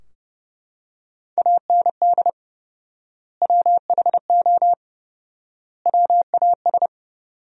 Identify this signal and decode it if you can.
Morse code, heard as a keyed tone, about 32 wpm, 700 Hz. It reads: AND WHO WAS
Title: 4x Word-Spacing - Sentences from Top 100 Words - 30wpm